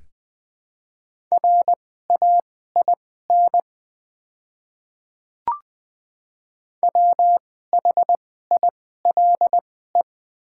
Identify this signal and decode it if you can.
Morse code, heard as a keyed tone, about 20 wpm, spaced as for 14 wpm, 700 Hz. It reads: RAIN WHILE
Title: 2x Character Spacing - Top 300 Words - 20wpm